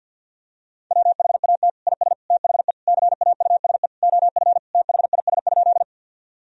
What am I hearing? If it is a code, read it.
Morse code, heard as a keyed tone, about 52 wpm, 700 Hz. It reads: WHAT IS THE CAUSE OF THIS?
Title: Sentences from Top 1000 Words - Review Part 1 - 50wpm